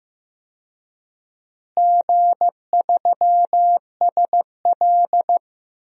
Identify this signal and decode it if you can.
Morse code, heard as a keyed tone, about 15 wpm, 700 Hz. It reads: G3SL
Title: DXCC-100 Call Signs - 4 Characters - Repeated 3x - 15wpm